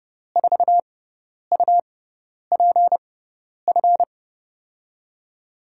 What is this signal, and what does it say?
Morse code, heard as a keyed tone, about 30 wpm, spaced as for 10 wpm, 700 Hz. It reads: 4UPF
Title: USA Call Signs - 6x Character Spacing - 30wpm